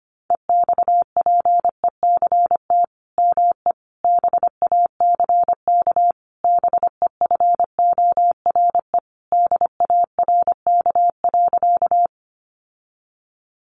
Morse code, heard as a keyed tone, about 25 wpm, 700 Hz. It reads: EXPECT ME BACK BEFORE DARK.
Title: Sentences from Top 1100 Words 25wpm